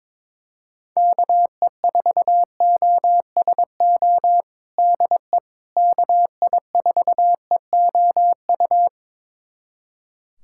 Morse code, heard as a keyed tone, about 22 wpm, 700 Hz. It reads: KE4OSO DE KI4EOU
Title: USA Callsign DE Callsign - 6 Characters - 22wpm